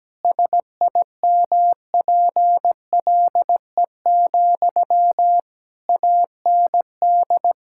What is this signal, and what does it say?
Morse code, heard as a keyed tone, about 17 wpm, 700 Hz. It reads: SIMPLE, AND
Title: Sentences from Top 1000 Words - Review Part 6 - 17wpm